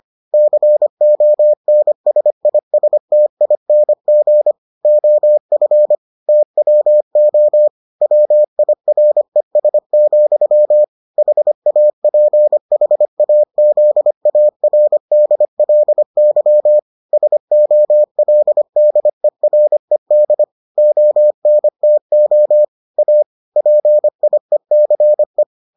Morse code, heard as a keyed tone, about 25 wpm, 600 Hz. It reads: CONSISTING OF TWO WIRES, HAPHAZARDLY SOLDERED ONTO A PIECE